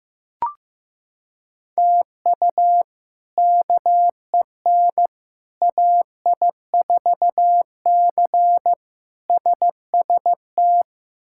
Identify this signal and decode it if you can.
Morse code, heard as a keyed tone, about 15 wpm, 700 Hz. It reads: TU KEN AI4C SST
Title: K1USN SST QSO Practice - 15wpm